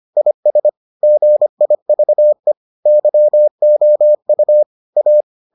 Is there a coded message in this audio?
Yes: Morse code from 0.2 to 5.2 s, heard as a keyed tone, about 25 wpm, 600 Hz. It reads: IS GIVE YOU A